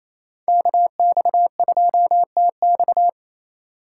Morse code, heard as a keyed tone, about 37 wpm, 700 Hz. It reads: KX2TX